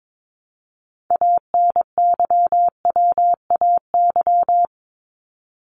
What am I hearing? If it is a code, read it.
Morse code, heard as a keyed tone, about 22 wpm, 700 Hz. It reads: ANYWAY
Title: Top 1501-1750 Words 22wpm